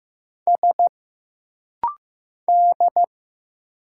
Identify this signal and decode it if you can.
Morse code, heard as a keyed tone, about 15 wpm, 700 Hz. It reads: S D
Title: Single Letters - Rapid-Fire 15wpm